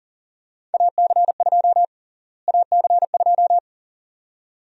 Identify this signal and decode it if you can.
Morse code, heard as a keyed tone, about 40 wpm, 700 Hz. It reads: AC2 AC2
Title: USA Call Sign Prefixes - Repeated 2x - 40wpm